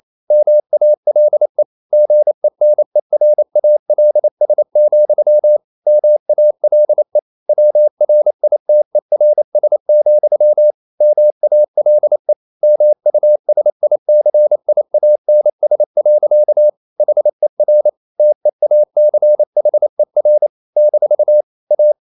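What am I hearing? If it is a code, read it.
Morse code, heard as a keyed tone, about 28 wpm, 600 Hz. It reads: MALE GENERALS, MALE WRITERS, MALE MUSICIANS. HER TEACHER - A
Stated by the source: Morse code practice at 28 wpm, text of Ken Follett's novel Never in English, from CW med Gnister.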